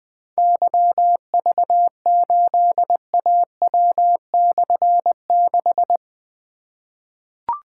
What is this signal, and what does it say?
Morse code, heard as a keyed tone, about 20 wpm, 700 Hz. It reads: YV8AW/6
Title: Difficult Call Signs 20wpm